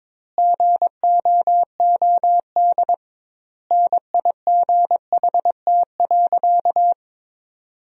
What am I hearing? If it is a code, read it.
Morse code, heard as a keyed tone, about 22 wpm, 700 Hz. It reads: GOOD NIGHT.
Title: 2x Word-Spacing - Sentences from Top 300 Words - 22wpm